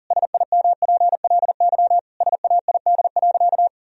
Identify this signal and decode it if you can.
Morse code, heard as a keyed tone, about 40 wpm, 700 Hz. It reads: SIMPLY SAID.
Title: Sentences from Top 1000 Words - Review Part 4 - 40wpm